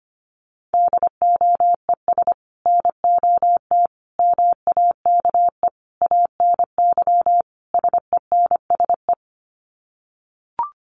Morse code, heard as a keyed tone, about 25 wpm, 700 Hz. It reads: DOES NOT MAKE ANY SENSE
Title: Sets of 5 Words Encore2 25wpm